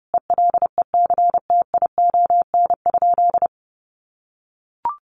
Morse code, heard as a keyed tone, about 30 wpm, 700 Hz. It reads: ELECTION?
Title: Sentences from Top 1500 Words 30wpm